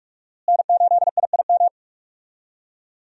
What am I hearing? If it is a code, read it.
Morse code, heard as a keyed tone, about 41 wpm, 700 Hz. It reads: N8IIM
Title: USA Call Signs - 5 Characters - Repeated 2x - 45wpm